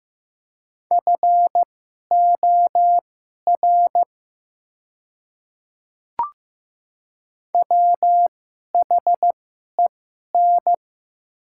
Morse code, heard as a keyed tone, about 15 wpm, spaced as for 11 wpm, 690 Hz. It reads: FOR WHEN
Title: 2x Character Spacing - Top 100 Words - 15wpm